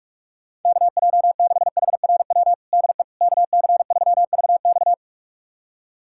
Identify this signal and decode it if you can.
Morse code, heard as a keyed tone, about 46 wpm, 700 Hz. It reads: KJ6HRW DE KC3VX
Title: USA Callsign DE Callsign 45wpm